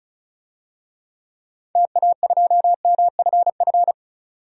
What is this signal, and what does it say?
Morse code, heard as a keyed tone, about 36 wpm, 700 Hz. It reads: TA2MFF
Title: DXCC-100 Call Signs - 6 Characters - 35wpm